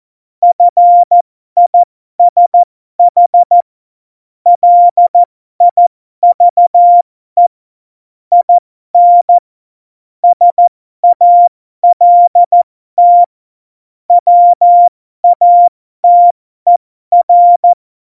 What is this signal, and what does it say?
Morse code, heard as a keyed tone, about 14 wpm, spaced as for 12 wpm, 700 Hz. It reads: FISH LIVE IN SALT WATER
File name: Morsekurs / Witze